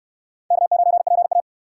Morse code, heard as a keyed tone, about 69 wpm, 700 Hz. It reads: K9PA